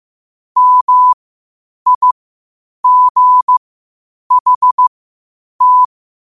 Morse code, heard as a keyed tone, about 15 wpm, spaced as for 8 wpm, 1000 Hz. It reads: MIGHT
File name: qotdfiles / 8wpm